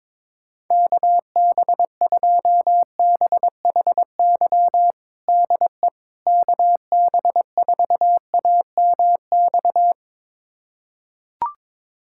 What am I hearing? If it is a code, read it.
Morse code, heard as a keyed tone, about 22 wpm, 700 Hz. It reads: KB2BHY DE KB4AMX
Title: USA Callsign DE Callsign - 6 Characters - 22wpm